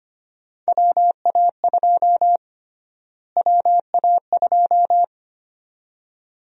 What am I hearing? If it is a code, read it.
Morse code, heard as a keyed tone, about 25 wpm, 700 Hz. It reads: WA2 WA2
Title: USA Call Sign Prefixes - Repeated 2x - 25wpm